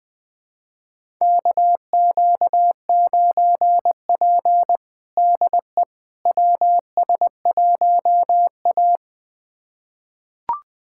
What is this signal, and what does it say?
Morse code, heard as a keyed tone, about 20 wpm, 700 Hz. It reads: KQ9P DE WS1A
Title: USA Callsign DE Callsign - 4 Characters - 20wpm